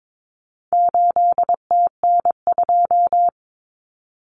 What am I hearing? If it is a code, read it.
Morse code, heard as a keyed tone, about 22 wpm, 700 Hz. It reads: 8TN2
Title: Common Cut Numbers - 4-Digit 22wpm